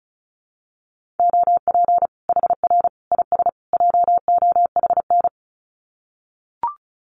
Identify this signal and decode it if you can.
Morse code, heard as a keyed tone, about 34 wpm, 700 Hz. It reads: OP HR IS JOHN
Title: Common QSO Phrases 35wpm